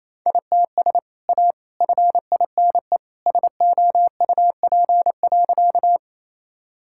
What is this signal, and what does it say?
Morse code, heard as a keyed tone, about 28 wpm, 700 Hz. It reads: ITS A FINE SOUP.
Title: Sentences from Top 1000 Words - Review Part 6 - 28wpm